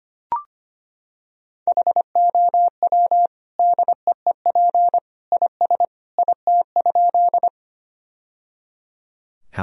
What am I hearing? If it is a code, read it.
Morse code, heard as a keyed tone, about 25 wpm, 700 Hz. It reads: HOW DEEP IS IT?
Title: Sentences from Top 1000 Words - Review Part 3 - 25wpm